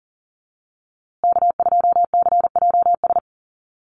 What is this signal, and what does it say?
Morse code, heard as a keyed tone, about 40 wpm, 700 Hz. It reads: K2CJS
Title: USA Call Signs Repeated 3x 40wpm